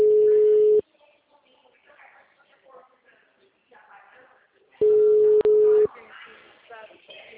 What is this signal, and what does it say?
Ringback tone (Europe, 425 Hz), with speech.
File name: out- -2050- - - .57433